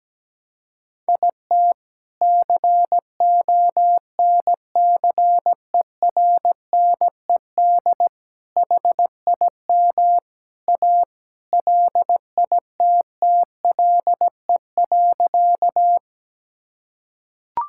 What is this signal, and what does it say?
Morse code, heard as a keyed tone, about 17 wpm, 700 Hz. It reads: IT CONCERNED HIM A LITTLE.
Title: Sentences from Top 1000 Words - Review Part 6 - 17wpm